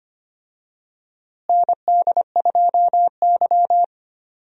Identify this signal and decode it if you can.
Morse code, heard as a keyed tone, about 25 wpm, 700 Hz. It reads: ND2Y
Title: USA Call Signs - 4 Characters - Repeated 2x - 25wpm